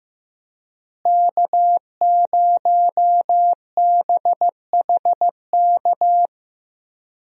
Morse code, heard as a keyed tone, about 15 wpm, 700 Hz. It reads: K0BHK
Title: USA Call Signs - 5 Characters - Repeated 3x - 15wpm